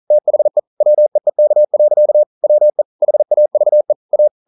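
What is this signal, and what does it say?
Morse code, heard as a keyed tone, about 43 wpm, 600 Hz. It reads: THE WEEK. WE HAVE A
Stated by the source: Morse code at 40 wpm, practice text, CW med Gnister.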